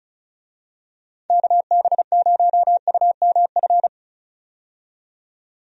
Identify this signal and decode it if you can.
Morse code, heard as a keyed tone, about 36 wpm, 700 Hz. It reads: KB0UMF
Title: USA Call Signs - 6 Characters - 35wpm